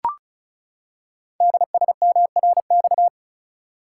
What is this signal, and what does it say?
Morse code, heard as a keyed tone, about 36 wpm, 700 Hz. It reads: DSMRX
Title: Code Groups - 5 Letters - 35wpm